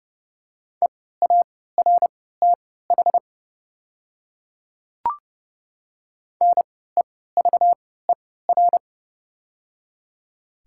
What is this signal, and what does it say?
Morse code, heard as a keyed tone, about 30 wpm, spaced as for 17 wpm, 700 Hz. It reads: EARTH NEVER
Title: 3x Character Spacing - Top 300 Words - 30wpm